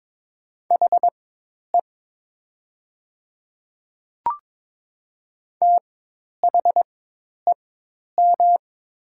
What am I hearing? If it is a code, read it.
Morse code, heard as a keyed tone, about 22 wpm, spaced as for 10 wpm, 700 Hz. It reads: HE THEM
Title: 4x Character Spacing - Top 100 Words - 22wpm